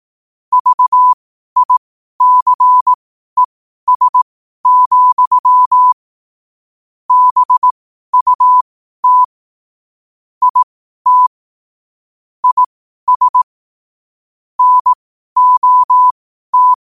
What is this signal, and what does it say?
Morse code, heard as a keyed tone, about 18 wpm, spaced as for 12 wpm, 1000 Hz. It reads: VICES, BUT IT IS NOT